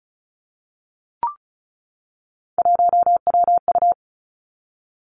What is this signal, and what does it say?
Morse code, heard as a keyed tone, about 36 wpm, 700 Hz. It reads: 1WU
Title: Three Challenging Letter-Number Combinations 35wpm